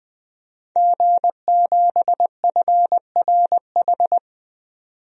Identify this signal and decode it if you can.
Morse code, heard as a keyed tone, about 20 wpm, 700 Hz. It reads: G7FRH
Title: DXCC-100 Call Signs - 5 Characters - 20wpm